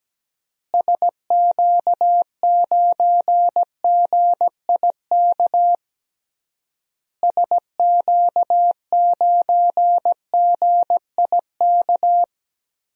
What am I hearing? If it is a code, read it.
Morse code, heard as a keyed tone, about 17 wpm, 700 Hz. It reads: SQ9GIK SQ9GIK
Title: DXCC-100 Call Signs - 6 Characters - Repeated 3x - 17wpm